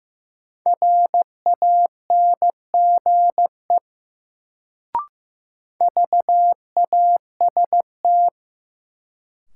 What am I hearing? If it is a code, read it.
Morse code, heard as a keyed tone, about 15 wpm, 700 Hz. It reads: RANGE VAST
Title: Sets of 4 Words - New Word Drill - 15wpm